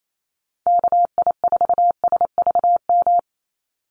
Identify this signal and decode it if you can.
Morse code, heard as a keyed tone, about 28 wpm, 700 Hz. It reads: KI4SVM
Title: NA SOTA Activator Callsigns 28wpm